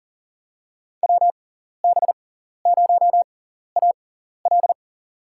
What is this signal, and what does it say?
Morse code, heard as a keyed tone, about 37 wpm, spaced as for 13 wpm, 700 Hz. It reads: WB0AL
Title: USA Call Signs - 5 Characters - 6x Character Spacing - 40wpm